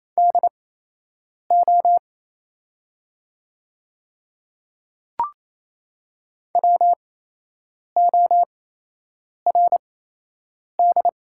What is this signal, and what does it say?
Morse code, heard as a keyed tone, about 28 wpm, spaced as for 8 wpm, 700 Hz. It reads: DO WORD
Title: 8x Character Spacing - Top 100 Words - 28wpm